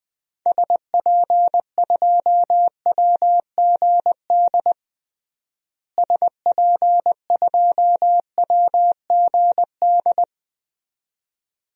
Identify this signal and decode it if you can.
Morse code, heard as a keyed tone, about 21 wpm, 700 Hz. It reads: SP2WGD SP2WGD
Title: DXCC-100 Call Signs - 6 Characters - Repeated 2x - 20wpm